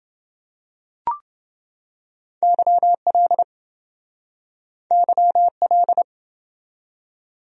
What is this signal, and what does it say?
Morse code, heard as a keyed tone, about 27 wpm, 700 Hz. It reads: YL YL